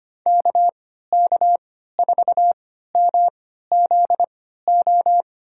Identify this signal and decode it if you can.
Morse code, heard as a keyed tone, about 25 wpm, spaced as for 14 wpm, 700 Hz. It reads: KK4MZO
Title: USA Call Signs - 6 Characters - 3x Character Spacing - 25wpm